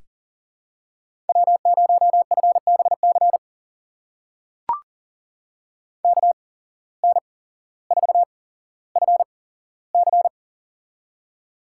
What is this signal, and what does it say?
Morse code, heard as a keyed tone, about 40 wpm, spaced as for 17 wpm, 700 Hz. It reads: W0FBC KN4FC
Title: USA Call Signs - 5 Characters - 8x Character Spacing - 40wpm